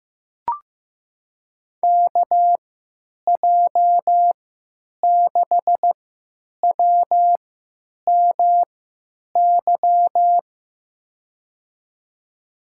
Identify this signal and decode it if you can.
Morse code, heard as a keyed tone, about 15 wpm, spaced as for 9 wpm, 700 Hz. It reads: KJ6WMY
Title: USA Call Signs - 6 Characters - 3x Character Spacing - 15wpm